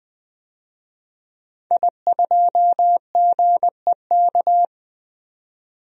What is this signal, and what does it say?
Morse code, heard as a keyed tone, about 21 wpm, 700 Hz. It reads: I2GEK
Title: DXCC-100 Call Signs - 5 Characters - 20wpm